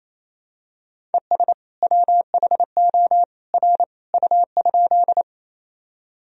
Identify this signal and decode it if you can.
Morse code, heard as a keyed tone, about 28 wpm, 700 Hz. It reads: ES WHO R U?
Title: Sentences from Top 100 Words and Abbreviations 28wpm